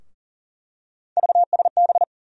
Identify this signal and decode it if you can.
Morse code, heard as a keyed tone, about 40 wpm, 700 Hz. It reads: VSB
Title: USA Call Sign Postfixes - 3 Letters 40wpm